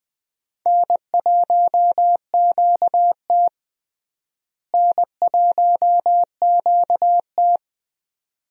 Morse code, heard as a keyed tone, about 20 wpm, 700 Hz. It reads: N1QT N1QT
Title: USA Call Signs - 4 Characters - Repeated 2x - 20wpm